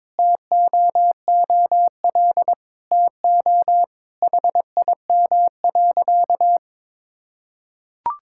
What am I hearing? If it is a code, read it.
Morse code, heard as a keyed tone, about 22 wpm, 700 Hz. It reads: TOOL TO HIM.
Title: Sentences from Top 1000 Words - Review Part 6 - 22wpm